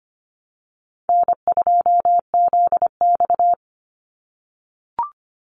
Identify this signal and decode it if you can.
Morse code, heard as a keyed tone, about 25 wpm, 700 Hz. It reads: N2ZX
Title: Call Signs - 25wpm